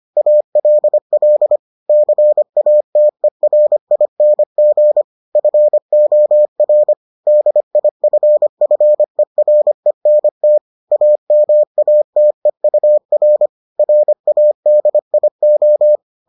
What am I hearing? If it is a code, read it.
Morse code, heard as a keyed tone, about 25 wpm, 600 Hz. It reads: ALL CATERING FOR DIFFERENT AMATEUR RADIO